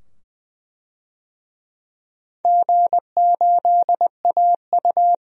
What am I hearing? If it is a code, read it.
Morse code, heard as a keyed tone, about 20 wpm, 700 Hz. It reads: G8AU
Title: DXCC-100 Call Signs - 4 Characters - 20wpm